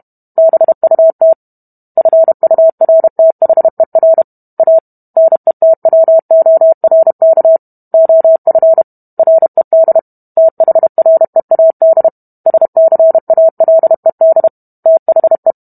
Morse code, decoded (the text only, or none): BUT FURTHER A NETWORK OF RED THREAD SCALED THE